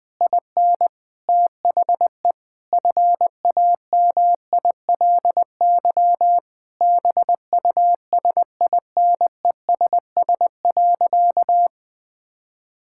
Morse code, heard as a keyed tone, about 20 wpm, 700 Hz. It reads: IN THE FAMILY BUSINESS.
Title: Sentences from Top 1000 Words - Review Part 6 - 20wpm